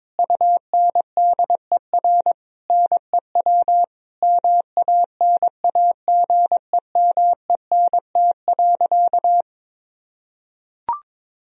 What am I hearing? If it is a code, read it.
Morse code, heard as a keyed tone, about 22 wpm, 700 Hz. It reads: UNDER NEW MANAGEMENT.
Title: Sentences from Top 1000 Words - Review Part 4 - 22wpm